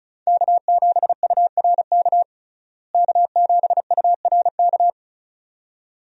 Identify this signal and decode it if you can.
Morse code, heard as a keyed tone, about 34 wpm, 700 Hz. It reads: K7URK K7URK
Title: USA Call Signs - 5 Characters - Repeated 2x - 35wpm